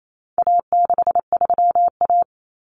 Morse code, heard as a keyed tone, about 28 wpm, 700 Hz. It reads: A63A
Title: Common Cut Numbers - 4-Digit 28wpm